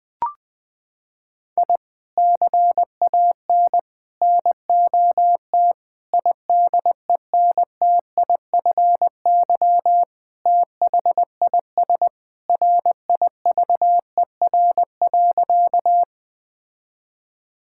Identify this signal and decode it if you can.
Morse code, heard as a keyed tone, about 20 wpm, 700 Hz. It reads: I CAN NOT IDENTIFY THIS RIVER.
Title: Sentences from Top 1000 Words - Review Part 6 - 20wpm